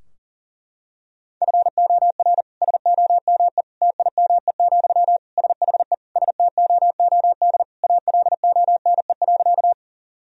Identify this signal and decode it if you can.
Morse code, heard as a keyed tone, about 40 wpm, 700 Hz. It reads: FOR SOME TIME, SHE STOOD ALONE.
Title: Sentences from Top 1000 Words - Review Part 2 - 40wpm